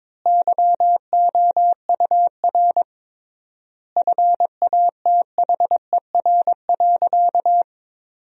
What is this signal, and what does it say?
Morse code, heard as a keyed tone, about 22 wpm, 700 Hz. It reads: YOUR FATHER.
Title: 3x Word-Spacing - Sentences from Top 300 Words - 22wpm